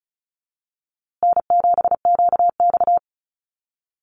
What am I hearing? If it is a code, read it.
Morse code, heard as a keyed tone, about 34 wpm, 700 Hz. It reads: N7QX
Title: USA Call Signs Repeated 3x 35wpm